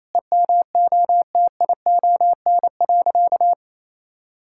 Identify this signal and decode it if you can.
Morse code, heard as a keyed tone, about 28 wpm, 700 Hz. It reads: EMOTION.
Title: Sentences from Top 1600 Words 28wpm